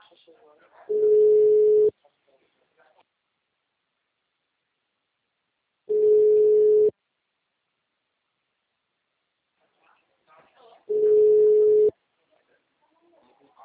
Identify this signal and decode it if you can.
Ringback tone (Europe, 425 Hz), with speech.